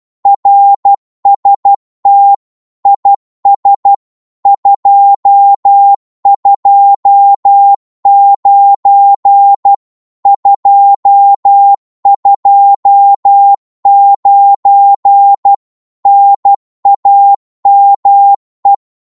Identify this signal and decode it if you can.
Morse code, heard as a keyed tone, about 12 wpm, 800 Hz. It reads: RST IS 229 229 NAME
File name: Morse / qso-12wpm